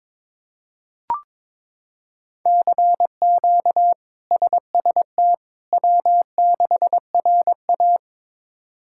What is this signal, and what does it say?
Morse code, heard as a keyed tone, about 22 wpm, 700 Hz. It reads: CQ SST W6RA
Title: K1USN SST QSO Practice - 22wpm